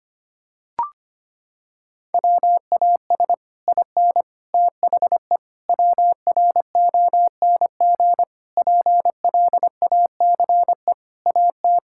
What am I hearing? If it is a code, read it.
Morse code, heard as a keyed tone, about 25 wpm, 700 Hz. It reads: WAS IN THE WRONG PLACE AT
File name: Sets of 9 Words - 25wpm